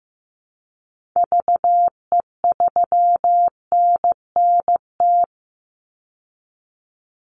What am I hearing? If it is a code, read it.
Morse code, heard as a keyed tone, about 15 wpm, 700 Hz. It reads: VE3NNT
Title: Call Signs Repeated 3x - 15wpm